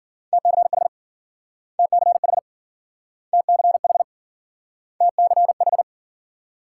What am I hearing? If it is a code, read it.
Morse code, heard as a keyed tone, about 52 wpm, 700 Hz. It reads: TCH TCH TCH TCH